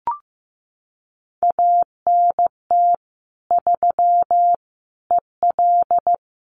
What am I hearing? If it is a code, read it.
Morse code, heard as a keyed tone, about 15 wpm, 700 Hz. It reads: ANT 3 EL
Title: Common QSO Phrases Encore 15wpm